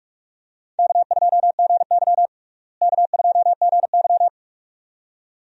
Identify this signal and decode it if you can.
Morse code, heard as a keyed tone, about 43 wpm, 700 Hz. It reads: K2GY K2GY
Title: USA Call Signs - 4 Characters - Repeated 2x - 45wpm